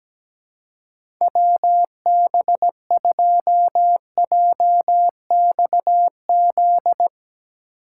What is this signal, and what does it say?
Morse code, heard as a keyed tone, about 17 wpm, 700 Hz. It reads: WB2JXZ
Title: USA Call Signs - 6 Characters - 2x Character Spacing - 17wpm